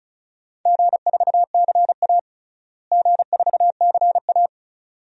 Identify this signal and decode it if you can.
Morse code, heard as a keyed tone, about 36 wpm, 700 Hz. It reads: G4CA G4CA
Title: DXCC-100 Call Signs - 4 Characters - Repeated 2x - 35wpm